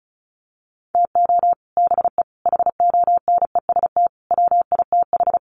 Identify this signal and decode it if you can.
Morse code, heard as a keyed tone, about 34 wpm, 700 Hz. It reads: TO BE HONEST WITH
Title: Sets of 5 Words Encore 35wpm